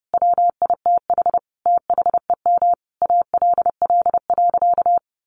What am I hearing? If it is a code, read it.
Morse code, heard as a keyed tone, about 30 wpm, 700 Hz. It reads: WITH THEM ALL.
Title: Sentences from Top 1400 Words 30wpm